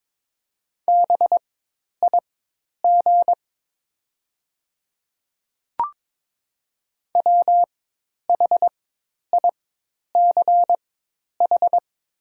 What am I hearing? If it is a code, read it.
Morse code, heard as a keyed tone, about 21 wpm, spaced as for 10 wpm, 700 Hz. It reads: BIG WHICH
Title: 4x Character Spacing - Top 200 Words - 22wpm